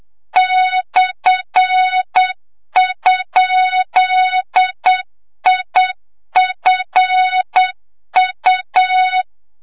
Morse code, heard as a keyed tone, about 8 wpm, 3700 Hz, with noise. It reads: /?IFU
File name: Morze-HA2MN / hang / 25 GYAKORLAT